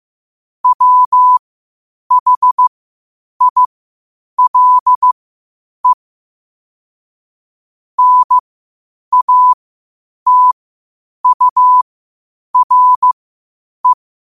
Morse code, heard as a keyed tone, about 15 wpm, spaced as for 8 wpm, 1000 Hz. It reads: WHILE NATURE
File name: qotdfiles / 8wpm